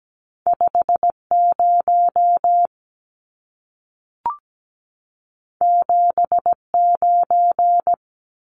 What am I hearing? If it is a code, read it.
Morse code, heard as a keyed tone, about 17 wpm, 700 Hz. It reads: 50 79
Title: Numbers - 2-Digit 17wpm